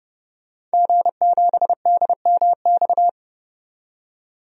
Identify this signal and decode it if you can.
Morse code, heard as a keyed tone, about 30 wpm, 700 Hz. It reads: G7DMX
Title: DXCC-100 Call Signs - 5 Characters - 30wpm